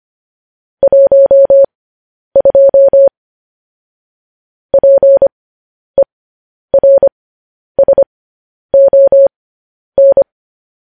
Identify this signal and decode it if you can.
Morse code, heard as a keyed tone, about 25 wpm, spaced as for 10 wpm, 550 Hz. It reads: 12 PERSON